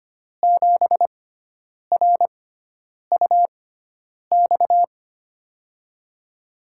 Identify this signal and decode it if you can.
Morse code, heard as a keyed tone, about 25 wpm, spaced as for 9 wpm, 700 Hz. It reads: 7RUX
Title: USA Call Signs - 6 Characters - 6x Character Spacing - 25wpm